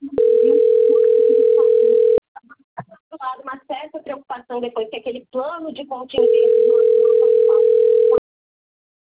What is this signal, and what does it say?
Ringback tone (North America), 440 + 480 Hz, with speech.
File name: rg-1600-unknown- - - .86752